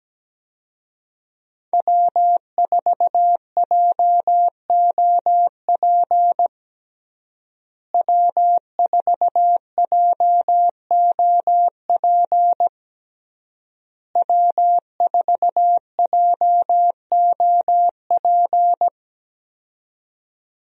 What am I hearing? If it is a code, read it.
Morse code, heard as a keyed tone, about 17 wpm, 700 Hz. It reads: W4JOP W4JOP W4JOP
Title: USA Call Signs - 5 Characters - Repeated 3x - 17wpm